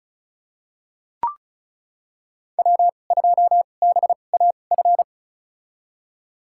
Morse code, heard as a keyed tone, about 36 wpm, spaced as for 25 wpm, 700 Hz. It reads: W2BAF
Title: USA Call Signs - 2x Character Spacing - 35wpm